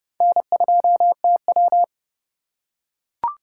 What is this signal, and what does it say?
Morse code, heard as a keyed tone, about 30 wpm, 700 Hz. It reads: N2TW